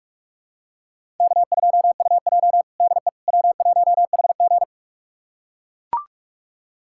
Morse code, heard as a keyed tone, about 44 wpm, 700 Hz. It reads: K2UJ DE W1HG
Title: USA Callsign DE Callsign - 4 Characters - 45wpm